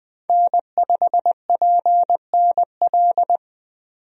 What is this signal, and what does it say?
Morse code, heard as a keyed tone, about 21 wpm, 700 Hz. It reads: N5PNL
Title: USA Call Signs - 5 Characters - 20wpm